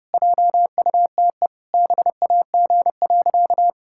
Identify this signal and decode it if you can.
Morse code, heard as a keyed tone, about 30 wpm, 700 Hz. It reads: JUTE BAG.
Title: Pangram Challenge v2 - 30wpm